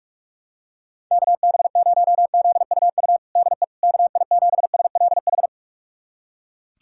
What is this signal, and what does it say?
Morse code, heard as a keyed tone, about 41 wpm, 700 Hz. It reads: KB0ZUU DE KI7SLH
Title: USA Callsign DE Callsign - 6 Characters - 45wpm